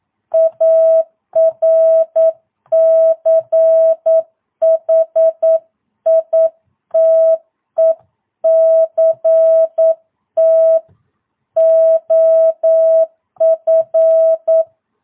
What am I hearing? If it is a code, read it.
Morse code, heard as a keyed tone, about 9 wpm, 650 Hz. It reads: ARCHITECT OF
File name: jh8cbh / 2022cw